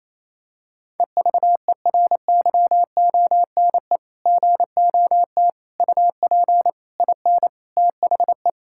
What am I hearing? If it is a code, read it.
Morse code, heard as a keyed tone, about 28 wpm, 700 Hz. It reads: EVERYONE GOT UP IN THE
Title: Sentences from Top 1000 Words - Review Part 5 - 28wpm